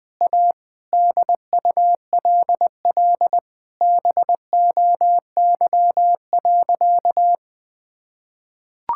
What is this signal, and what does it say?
Morse code, heard as a keyed tone, about 21 wpm, 700 Hz. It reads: A DULL BOY.
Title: Sentences with 7-Word Phrases - 20wpm